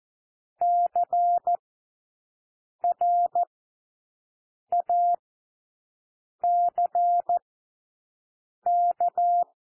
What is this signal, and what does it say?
Morse code, heard as a keyed tone, about 14 wpm, spaced as for 6 wpm, 700 Hz. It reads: CRACK
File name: cwmp3 / W1 SOUND WORDS 14 wpm 0